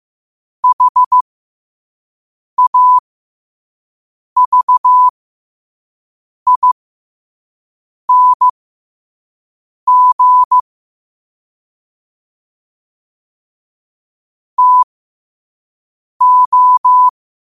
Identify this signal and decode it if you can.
Morse code, heard as a keyed tone, about 15 wpm, spaced as for 5 wpm, 1000 Hz. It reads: HAVING TO
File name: qotdfiles / 5wpm